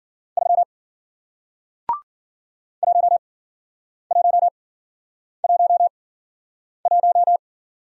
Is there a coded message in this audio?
Yes: Morse code from 0.4 to 7.4 s, heard as a keyed tone, about 55 wpm, 700 Hz. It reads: 3 1 1 1 1